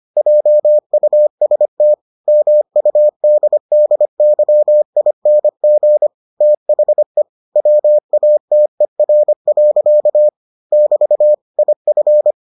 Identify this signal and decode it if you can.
Morse code, heard as a keyed tone, about 25 wpm, 600 Hz. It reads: JUST MUDDYING THE WATER. <BT> IF